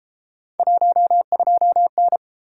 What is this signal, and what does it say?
Morse code, heard as a keyed tone, about 31 wpm, 700 Hz. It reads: 12N